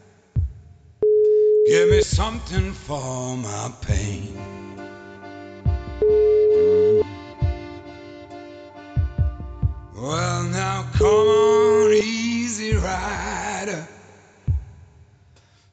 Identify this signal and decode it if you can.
Ringback tone (Europe, 425 Hz).